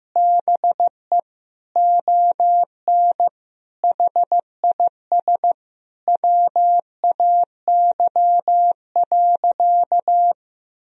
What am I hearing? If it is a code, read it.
Morse code, heard as a keyed tone, about 15 wpm, 700 Hz. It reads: BE ON HIS WAY.
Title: Sentences from Top 300 Words and Abbreviations 15wpm